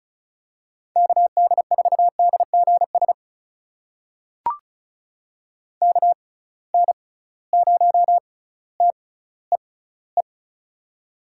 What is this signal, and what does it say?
Morse code, heard as a keyed tone, about 36 wpm, spaced as for 18 wpm, 700 Hz. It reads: KD4DGS KN0TEE
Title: USA Call Signs - 6 Characters - 6x Character Spacing - 35wpm